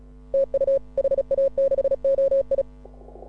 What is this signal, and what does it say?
Morse code, heard as a keyed tone, about 36 wpm, 560 Hz. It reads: TU HA6OI
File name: AUDIO2018 / HA6OI CW